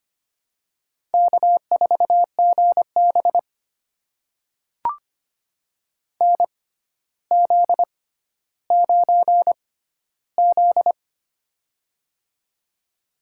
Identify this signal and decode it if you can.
Morse code, heard as a keyed tone, about 25 wpm, spaced as for 13 wpm, 700 Hz. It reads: K4GB NZ9Z
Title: USA Call Signs - 4 Characters - 6x Character Spacing - 25wpm